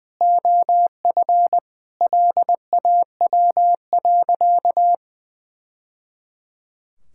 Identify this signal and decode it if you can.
Morse code, heard as a keyed tone, about 21 wpm, 700 Hz. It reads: OF LAW.
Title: Sentences from Top 1000 Words - Review Part 5 - 20wpm